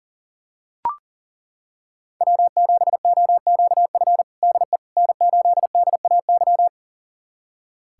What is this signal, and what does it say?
Morse code, heard as a keyed tone, about 40 wpm, 700 Hz. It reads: W7OQF DE N8DAY
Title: USA Callsign DE Callsign - 5 Characters - 40wpm